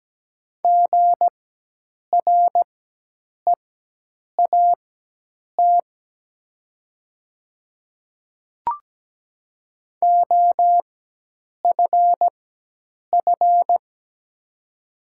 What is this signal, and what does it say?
Morse code, heard as a keyed tone, about 17 wpm, spaced as for 8 wpm, 690 Hz. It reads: GREAT OFF
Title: 4x Character Spacing - Top 300 Words - 17wpm